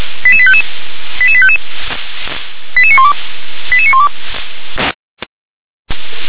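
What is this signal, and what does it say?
Five-tone selective call (ZVEI), heard as quick runs of five tones, tones of about 70 ms. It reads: 88855 88855 88912 88912
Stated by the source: ZVEI five-tone selective call received on an SDR.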